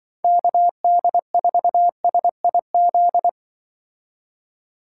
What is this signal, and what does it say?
Morse code, heard as a keyed tone, about 23 wpm, 700 Hz. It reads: KD4SIZ
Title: Speed Racing - USA 2x3 Call Signs 20wpm